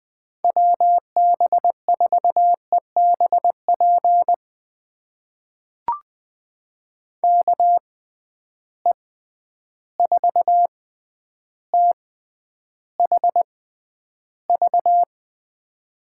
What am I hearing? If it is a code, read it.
Morse code, heard as a keyed tone, about 20 wpm, spaced as for 10 wpm, 700 Hz. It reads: WB4EBP KE4THV
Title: USA Call Signs - 6 Characters - 6x Character Spacing - 20wpm